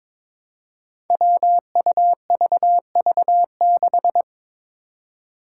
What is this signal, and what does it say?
Morse code, heard as a keyed tone, about 22 wpm, 700 Hz. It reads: WUVV6
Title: Five Challenging Letter-Number Combinations 22wpm